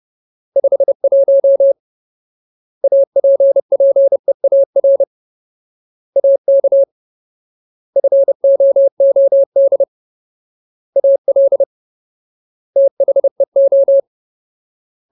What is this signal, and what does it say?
Morse code, heard as a keyed tone, about 30 wpm, 550 Hz. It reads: 51 APPEAR AK FOOD AL THEO
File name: Ultimate Word List / 30WPM Ultimate